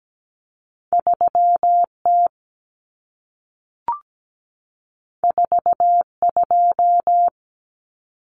Morse code, heard as a keyed tone, about 17 wpm, 700 Hz. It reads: 3T 42
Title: Common Cut Numbers - 2-Digit 17wpm